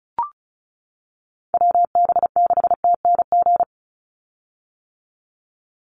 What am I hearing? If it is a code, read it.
Morse code, heard as a keyed tone, about 34 wpm, 700 Hz. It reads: WB6TNG
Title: USA Call Signs 35wpm